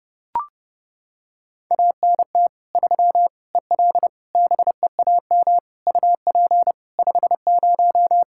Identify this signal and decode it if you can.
Morse code, heard as a keyed tone, about 30 wpm, 700 Hz. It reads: ANT 3 EL BEAM UP 50
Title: Practice QSO - Exchange 2 - Info - 30wpm